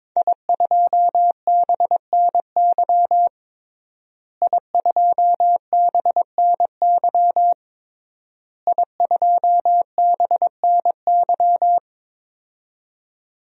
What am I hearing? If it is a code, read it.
Morse code, heard as a keyed tone, about 22 wpm, 700 Hz. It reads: I2BNY I2BNY I2BNY
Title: DXCC-100 Call Signs - 5 Characters - Repeated 3x - 22wpm